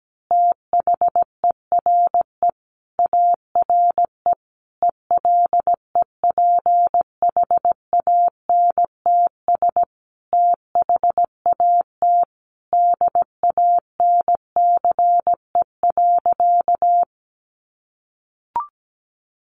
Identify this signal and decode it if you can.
Morse code, heard as a keyed tone, about 17 wpm, 700 Hz. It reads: THERE ARE ELEPHANTS THAT DANCE.
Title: Easy Phrases 17wpm